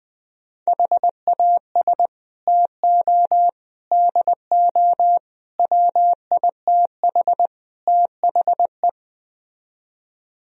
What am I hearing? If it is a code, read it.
Morse code, heard as a keyed tone, about 20 wpm, 700 Hz. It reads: HAS TO DO WITH THE
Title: Speed-Racing - Sets of 5 Words 17wpm